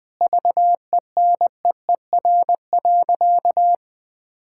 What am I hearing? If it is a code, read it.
Morse code, heard as a keyed tone, about 20 wpm, 700 Hz. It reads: VENEER.
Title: Pangram Challenge v2 - 20wpm